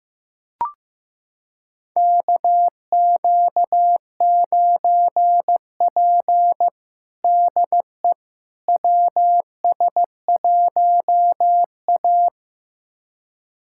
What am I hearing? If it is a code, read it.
Morse code, heard as a keyed tone, about 15 wpm, 700 Hz. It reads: KQ9P DE WS1A
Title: USA Callsign DE Callsign - 4 Characters - 15wpm